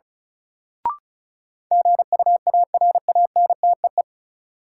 Morse code, heard as a keyed tone, about 36 wpm, 700 Hz. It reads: GUARANTEE
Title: Sentences from Top 1600 Words - New Word Drill Alt - 35wpm